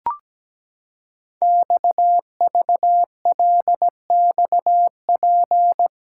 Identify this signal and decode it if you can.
Morse code, heard as a keyed tone, about 17 wpm, 700 Hz. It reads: XVLXP
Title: Code Groups - 5 Letters - 17wpm